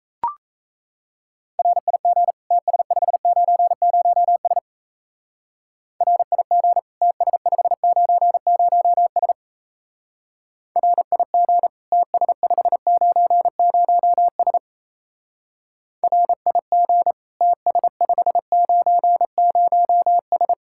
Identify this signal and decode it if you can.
Morse code, heard as a keyed tone, about 35 wpm, 700 Hz. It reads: RIG TS590S RIG TS590S RIG TS590S RIG TS590S
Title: Speed-Racing Common QSO Phrases - 28wpm